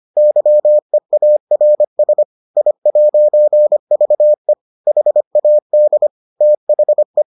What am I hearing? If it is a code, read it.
Morse code, heard as a keyed tone, about 25 wpm, 600 Hz. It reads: YEARS I'VE HAD THE